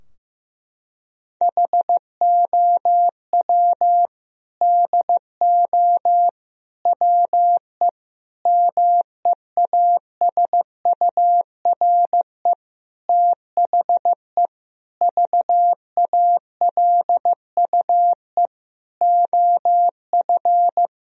Morse code, heard as a keyed tone, about 15 wpm, 700 Hz. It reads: HOW DO WE MEASURE THE VALUE OF
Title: Sentences from Top 1000 Words - Review Part 3 - 15wpm